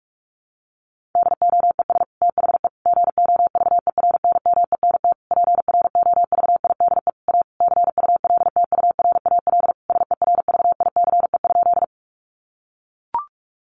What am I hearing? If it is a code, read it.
Morse code, heard as a keyed tone, about 44 wpm, 700 Hz. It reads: DOES THE GOVERNMENT PROVIDE A CULTURAL SERVICE?
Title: Sentences from Top 1500 Words 45wpm